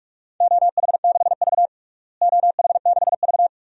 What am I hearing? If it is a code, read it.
Morse code, heard as a keyed tone, about 47 wpm, 700 Hz. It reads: OH6V OH6V
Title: DXCC-100 Call Signs - 4 Characters - Repeated 2x - 45wpm